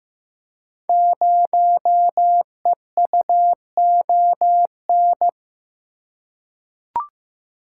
Morse code, heard as a keyed tone, about 15 wpm, 700 Hz. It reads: 0EUON
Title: Code Groups - 5 Letter:Numbers - 15wpm